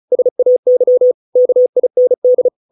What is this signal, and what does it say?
Morse code, heard as a keyed tone, about 34 wpm, 500 Hz. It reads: SAY KIND